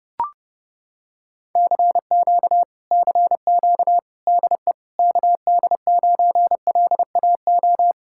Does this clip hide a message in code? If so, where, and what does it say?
Yes: Morse code from 1.6 to 7.9 s, heard as a keyed tone, about 30 wpm, 700 Hz. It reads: CQ CQ DE KD9LAO